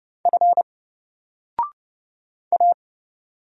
Morse code, heard as a keyed tone, about 30 wpm, 700 Hz. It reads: F A
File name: Single-Letter-Number-30wpm